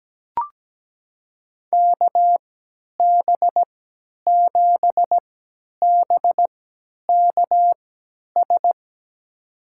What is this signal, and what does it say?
Morse code, heard as a keyed tone, about 17 wpm, spaced as for 10 wpm, 700 Hz. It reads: KB7BKS
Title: USA Call Signs - 3x Character Spacing - 17wpm